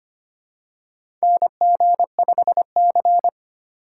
Morse code, heard as a keyed tone, about 25 wpm, 700 Hz. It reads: NG5C
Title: USA Call Signs - 4 Characters - 3x Character Spacing - 25wpm